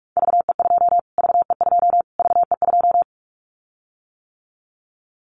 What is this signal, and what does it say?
Morse code, heard as a keyed tone, about 44 wpm, 700 Hz. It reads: VE2 VE2 VE2
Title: Callsign Prefixes for Top 25 DXCC - Repeated 3x - 45wpm